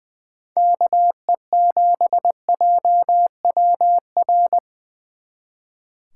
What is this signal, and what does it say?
Morse code, heard as a keyed tone, about 24 wpm, 700 Hz. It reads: KE7JWR
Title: Speed Racing - USA Call Signs 20wpm